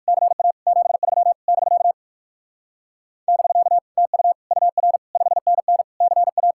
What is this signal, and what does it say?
Morse code, heard as a keyed tone, about 64 wpm, spaced as for 55 wpm, 700 Hz. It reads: CA 73 <BK> <BK> TU UR 5NN CA
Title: Speed-Racing Common QSO Phrases - 45wpm